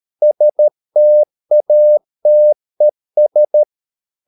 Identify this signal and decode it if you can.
Morse code, heard as a keyed tone, about 13 wpm, 600 Hz. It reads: STATES